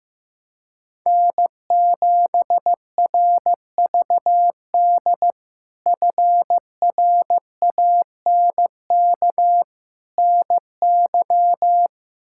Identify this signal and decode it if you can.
Morse code, heard as a keyed tone, about 16 wpm, 700 Hz. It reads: N7RVD FRANK NY
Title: K1USN SST QSO Practice - 15wpm